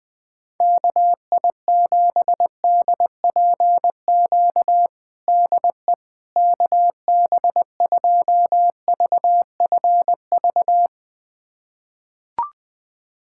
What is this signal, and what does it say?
Morse code, heard as a keyed tone, about 20 wpm, 700 Hz. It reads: KI7DPQ DE KB2VFV
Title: USA Callsign DE Callsign - 6 Characters - 20wpm